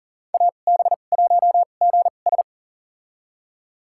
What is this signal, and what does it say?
Morse code, heard as a keyed tone, about 40 wpm, spaced as for 29 wpm, 700 Hz. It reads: AB1GS
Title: USA Call Signs - 5 Characters - 2x Character Spacing - 40wpm